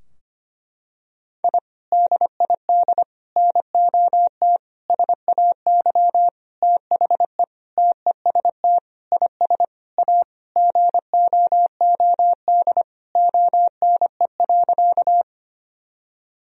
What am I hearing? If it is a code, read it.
Morse code, heard as a keyed tone, about 25 wpm, 700 Hz. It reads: I DID NOT SAY THE TEST IS A GOOD ONE.
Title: Sentences from Top 1000 Words - Review Part 3 - 25wpm